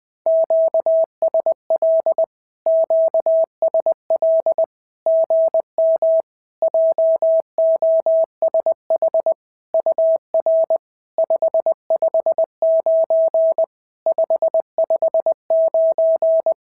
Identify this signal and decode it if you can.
Morse code, heard as a keyed tone, about 18 wpm, 650 Hz. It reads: QSL QSL GM JOSH UR 559 559
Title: SOTA QSO Practice 20wpm